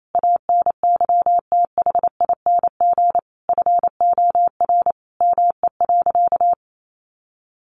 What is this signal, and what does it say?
Morse code, heard as a keyed tone, about 28 wpm, 700 Hz. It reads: ANYTHING FOR ME.
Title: Sentences from Top 1000 Words 28wpm